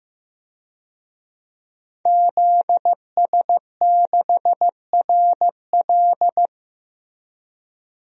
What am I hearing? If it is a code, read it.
Morse code, heard as a keyed tone, about 15 wpm, 700 Hz. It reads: ZS6RL
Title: DXCC-100 Call Signs - 5 Characters - Repeated 3x - 15wpm